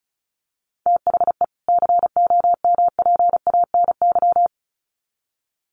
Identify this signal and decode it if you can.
Morse code, heard as a keyed tone, about 34 wpm, 700 Hz. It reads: THE COMPANY
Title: Sets of 2 Words 35wpm